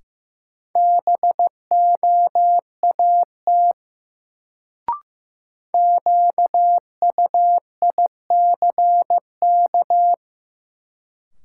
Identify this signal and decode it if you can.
Morse code, heard as a keyed tone, about 15 wpm, 700 Hz. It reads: BOAT QUICK
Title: Sentences from Top 500 Words - New Word Drill - 15wpm